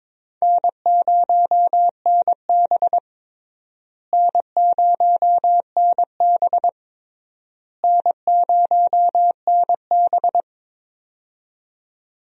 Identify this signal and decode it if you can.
Morse code, heard as a keyed tone, about 22 wpm, 700 Hz. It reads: N0NB N0NB N0NB
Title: USA Call Signs - 4 Characters - Repeated 3x - 22wpm